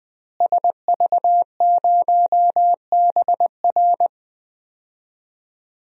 Morse code, heard as a keyed tone, about 19 wpm, 700 Hz. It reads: SV0BR
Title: DXCC-100 Call Signs - 5 Characters - Repeated 3x - 20wpm